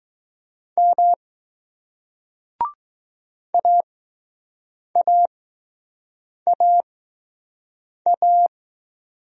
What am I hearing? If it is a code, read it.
Morse code, heard as a keyed tone, about 19 wpm, spaced as for 6 wpm, 700 Hz. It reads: M AAAA